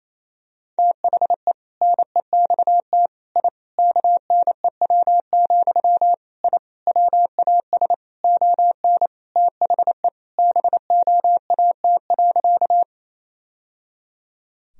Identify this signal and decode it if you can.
Morse code, heard as a keyed tone, about 28 wpm, 700 Hz. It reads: THE NEXT I KNEW, I WAS ON THE BOAT.
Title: Sentences from Top 1000 Words - Review Part 3 - 28wpm